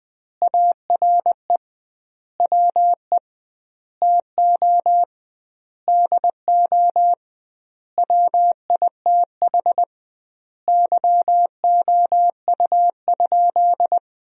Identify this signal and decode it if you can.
Morse code, heard as a keyed tone, about 21 wpm, 700 Hz. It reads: ARE WE TO DO WITH YOU?
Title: 2x Word-Spacing - Sentences from Top 100 Words - 20wpm